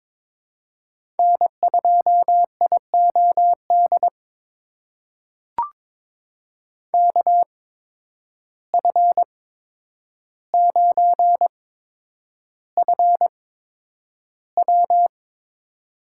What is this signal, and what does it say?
Morse code, heard as a keyed tone, about 22 wpm, spaced as for 9 wpm, 700 Hz. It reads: N2IOD KF9FW
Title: USA Call Signs - 5 Characters - 8x Character Spacing - 22wpm